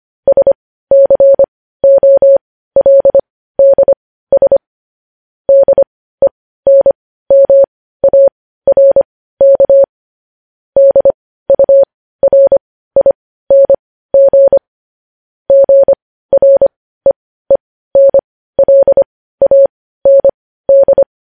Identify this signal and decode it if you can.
Morse code, heard as a keyed tone, about 25 wpm, spaced as for 15 wpm, 550 Hz. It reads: SCOLDS DENMARK DURING GREENLAND
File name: podcast / News headlines-1x-25x15